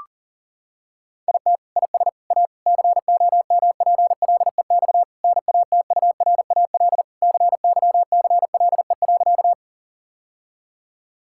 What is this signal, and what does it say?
Morse code, heard as a keyed tone, about 40 wpm, 700 Hz. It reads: IT IS A COMPLEX NATURAL CYCLE.
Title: Sentences from Top 1000 Words - Review Part 6 - 40wpm